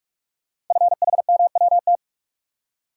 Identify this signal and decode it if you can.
Morse code, heard as a keyed tone, about 44 wpm, 690 Hz. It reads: FHMWT
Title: Code Groups - 5 Letters - 45wpm